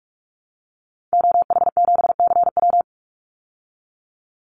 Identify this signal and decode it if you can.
Morse code, heard as a keyed tone, about 44 wpm, 700 Hz. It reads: OH7CW
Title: Call Signs Repeated 3x - 45wpm